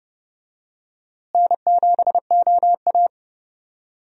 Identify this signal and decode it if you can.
Morse code, heard as a keyed tone, about 32 wpm, 700 Hz. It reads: N7OA
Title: USA Call Signs - 4 Characters - Repeated 3x - 30wpm